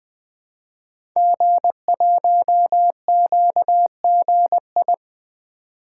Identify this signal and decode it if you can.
Morse code, heard as a keyed tone, about 20 wpm, 700 Hz. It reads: G1QGI
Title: Code Groups - 5 Letter:Numbers - 20wpm